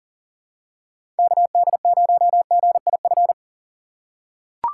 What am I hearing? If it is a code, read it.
Morse code, heard as a keyed tone, about 40 wpm, 700 Hz. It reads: KD0GIF
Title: USA Call Signs - 6 Characters - 2x Character Spacing - 40wpm